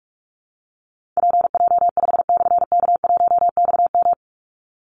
Morse code, heard as a keyed tone, about 44 wpm, 700 Hz. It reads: PJ5/K1XM
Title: Call Signs Repeated 3x - 45wpm